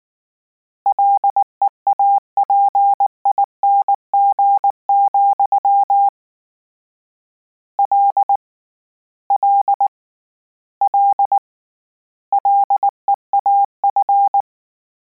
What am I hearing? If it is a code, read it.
Morse code, heard as a keyed tone, about 19 wpm, 800 Hz. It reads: LEAPING, L L L LEAF